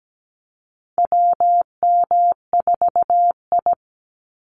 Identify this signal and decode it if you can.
Morse code, heard as a keyed tone, about 17 wpm, 700 Hz. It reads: WM4I
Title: Call Signs Repeated 3x - 17wpm